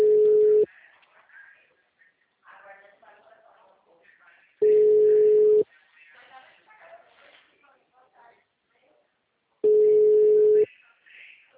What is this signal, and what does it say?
Ringback tone (Europe, 425 Hz), with speech.